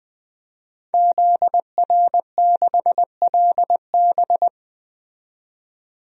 Morse code, heard as a keyed tone, about 20 wpm, 700 Hz. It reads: ZR6LB
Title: DXCC-100 Call Signs - 5 Characters - 20wpm